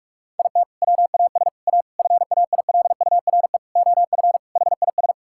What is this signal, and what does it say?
Morse code, heard as a keyed tone, about 46 wpm, 700 Hz. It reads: IT WAS A FAILURE OF HIS
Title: Sentences from Top 1000 Words - Review Part 6 - 45wpm